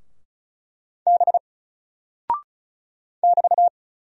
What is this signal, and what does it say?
Morse code, heard as a keyed tone, about 33 wpm, 700 Hz. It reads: B <BT>
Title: Single Character (Letter:Number:Punctuation:Prosign) 35wpm